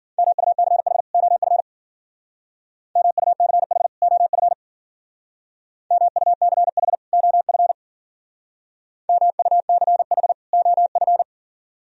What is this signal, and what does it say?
Morse code, heard as a keyed tone, about 50 wpm, 690 Hz. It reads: MUCH OF MUCH OF MUCH OF MUCH OF
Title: Speed-Racing - Sets of 2 Words 40wpm